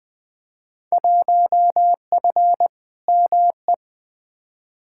Morse code, heard as a keyed tone, about 20 wpm, 700 Hz. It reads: 1F ME
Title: Field Day Exchanges Encore - 20wpm